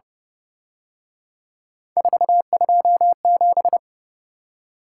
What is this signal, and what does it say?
Morse code, heard as a keyed tone, about 29 wpm, 700 Hz. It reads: 427